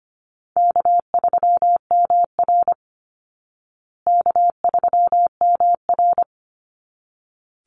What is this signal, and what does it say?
Morse code, heard as a keyed tone, about 25 wpm, 700 Hz. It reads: K3MR K3MR
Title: Call Signs Repeated 3x - 25wpm